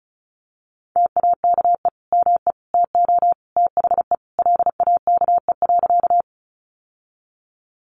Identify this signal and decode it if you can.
Morse code, heard as a keyed tone, about 34 wpm, 700 Hz. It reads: TAKE ME TO THE LAKE.
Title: Sentences from Top 1100 Words 35wpm